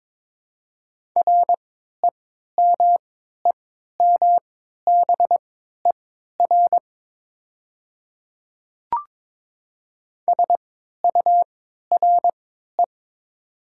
Morse code, heard as a keyed tone, about 22 wpm, spaced as for 13 wpm, 700 Hz. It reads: REMEMBER SURE
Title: 3x Character Spacing - Top 500 Words - 22wpm